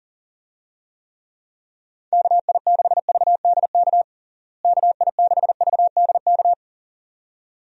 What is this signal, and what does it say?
Morse code, heard as a keyed tone, about 40 wpm, 700 Hz. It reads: KI6VDK KI6VDK
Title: USA Call Signs - 6 Characters - Repeated 2x - 40wpm